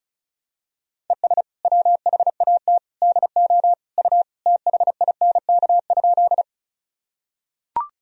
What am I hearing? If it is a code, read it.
Morse code, heard as a keyed tone, about 36 wpm, 700 Hz. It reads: ES WHAT DO U THINK?
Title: Sentences from Top 200 Words and Abbreviations 35wpm